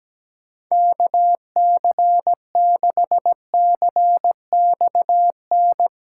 Morse code, heard as a keyed tone, about 17 wpm, 700 Hz. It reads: KC6CXN
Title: USA Callsign DE Callsign - 6 Characters - 17wpm